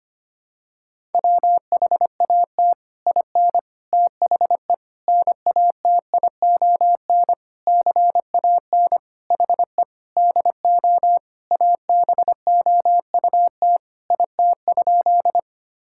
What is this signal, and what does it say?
Morse code, heard as a keyed tone, about 25 wpm, 700 Hz. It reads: WHAT IN THE NATION CAN HE DO ABOUT IT?
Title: Sentences from Top 1000 Words - Review Part 6 - 25wpm